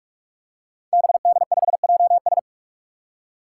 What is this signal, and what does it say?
Morse code, heard as a keyed tone, about 41 wpm, 700 Hz. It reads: BD5JS
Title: Five Challenging Letter-Number Combinations 45wpm